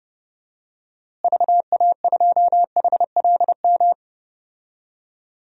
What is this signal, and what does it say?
Morse code, heard as a keyed tone, about 32 wpm, 700 Hz. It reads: VA2HLM
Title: DXCC-100 Call Signs - 6 Characters - Repeated 3x - 30wpm